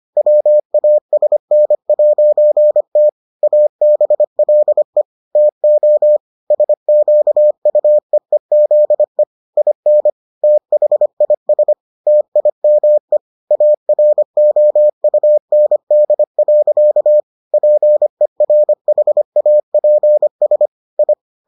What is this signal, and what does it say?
Morse code, heard as a keyed tone, about 25 wpm, 600 Hz. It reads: WASN'T ABLE TO SQUEEZE IN THIS TIME AROUND. PERHAPS I